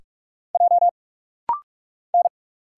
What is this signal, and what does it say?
Morse code, heard as a keyed tone, about 46 wpm, 700 Hz. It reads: J N
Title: Single Letters - Rapid-Fire 45wpm